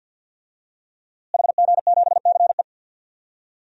Morse code, heard as a keyed tone, about 52 wpm, 700 Hz. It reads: HG7CE
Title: DXCC-100 Call Signs - 5 Characters - Repeated 3x - 50wpm